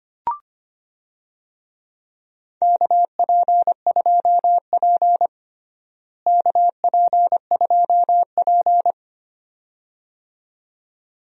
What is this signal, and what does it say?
Morse code, heard as a keyed tone, about 25 wpm, 700 Hz. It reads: KP2P KP2P
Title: DXCC-100 Call Signs - 4 Characters - Repeated 2x - 25wpm